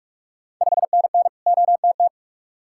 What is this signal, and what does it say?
Morse code, heard as a keyed tone, about 41 wpm, 700 Hz. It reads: 5NN OTT
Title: Ontario QSO Party Exchange 45wpm